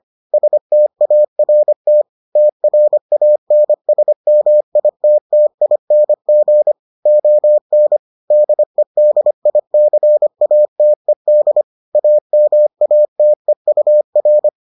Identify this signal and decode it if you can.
Morse code, heard as a keyed tone, about 25 wpm, 600 Hz. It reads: START TRANSMITTING ON DEDICATED AMATEUR